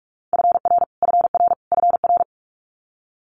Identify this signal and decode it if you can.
Morse code, heard as a keyed tone, about 46 wpm, 700 Hz. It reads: FR FR FR
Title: Multi-letter Phonemes Repeated 3x 45wpm